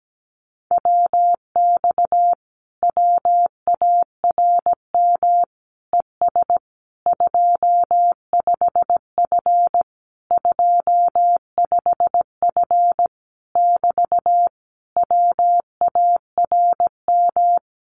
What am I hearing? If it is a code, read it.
Morse code, heard as a keyed tone, about 17 wpm, 700 Hz. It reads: WX WARM ES 25F 25F <BT> WARM
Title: Common QSO Phrases Encore 17wpm